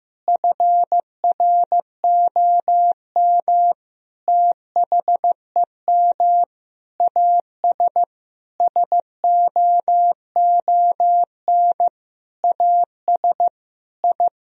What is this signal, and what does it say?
Morse code, heard as a keyed tone, about 15 wpm, 700 Hz. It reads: FROM THEM AS SOON AS I
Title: Sentences from Top 1000 Words - Review Part 3 - 15wpm